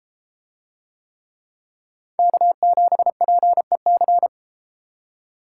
Morse code, heard as a keyed tone, about 32 wpm, 700 Hz. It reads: K7PEC